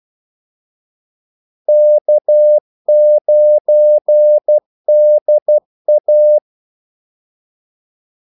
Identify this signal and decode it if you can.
Morse code, heard as a keyed tone, about 12 wpm, 600 Hz. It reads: K9DA
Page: 2 ekstra ordmellemrum mellem kaldesignalerne